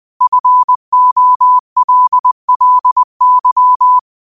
Morse code, heard as a keyed tone, about 20 wpm, 1000 Hz. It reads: FOLLY
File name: qotdfiles / 20wpm